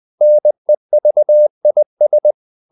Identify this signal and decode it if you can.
Morse code, heard as a keyed tone, about 20 wpm, 600 Hz. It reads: NEVIS
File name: CW / Practice / NationalCapitals20